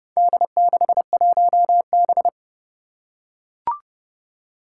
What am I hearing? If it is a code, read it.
Morse code, heard as a keyed tone, about 30 wpm, 700 Hz. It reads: D61B
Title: Four Challenging Letter-Number Combinations 30wpm